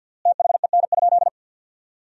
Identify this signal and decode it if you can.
Morse code, heard as a keyed tone, about 50 wpm, 700 Hz. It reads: THEN?
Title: 3x Word-Spacing - Sentences from Top 500 Words - 50wpm